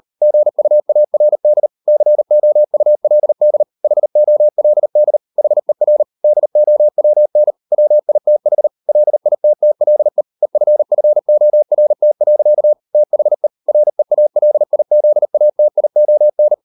Morse code, heard as a keyed tone, about 38 wpm, 600 Hz. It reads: GUARD COULD HOLD HER DOWN WITH LITTLE EFFORT. THE REALIZATION